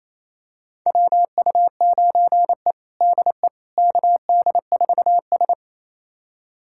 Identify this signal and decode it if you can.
Morse code, heard as a keyed tone, about 28 wpm, 700 Hz. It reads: WU9E DE KD4S
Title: USA Callsign DE Callsign - 4 Characters - 28wpm